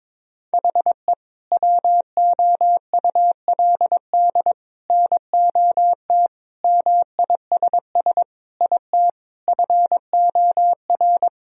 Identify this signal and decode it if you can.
Morse code, heard as a keyed tone, about 22 wpm, 700 Hz. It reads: HE WOULD NOT MISS IT FOR
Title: Sentences from Top 1000 Words - Review Part 3 - 22wpm